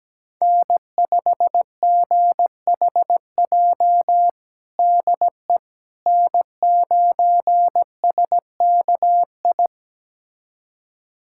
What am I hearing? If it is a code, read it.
Morse code, heard as a keyed tone, about 17 wpm, 700 Hz. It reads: N5GHJ DE N9SKI
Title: USA Callsign DE Callsign - 5 Characters - 17wpm